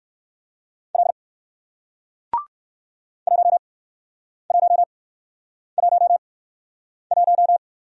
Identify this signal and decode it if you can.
Morse code, heard as a keyed tone, about 61 wpm, spaced as for 9 wpm, 700 Hz. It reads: L 1111